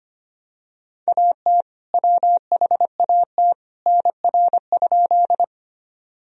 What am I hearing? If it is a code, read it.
Morse code, heard as a keyed tone, about 25 wpm, 700 Hz. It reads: AT WHAT NR?
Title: Sentences from Top 100 Words and Abbreviations 25wpm